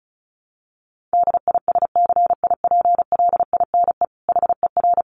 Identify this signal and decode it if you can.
Morse code, heard as a keyed tone, about 34 wpm, 700 Hz. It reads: DISCIPLINE HER
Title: Sentences from Top 900 Words 35wpm